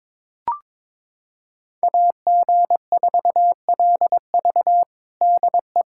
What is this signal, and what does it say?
Morse code, heard as a keyed tone, about 22 wpm, 700 Hz. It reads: AG4LV DE
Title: USA Callsign DE Callsign 22wpm